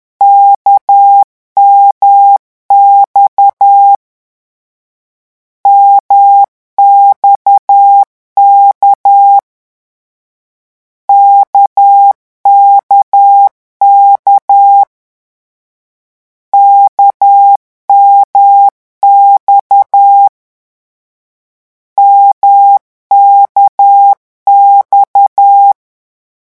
Morse code, heard as a keyed tone, about 10 wpm, 800 Hz. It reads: KMX MXK KKK KMX MKX
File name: morse mp3 / lecon1-vitesse 10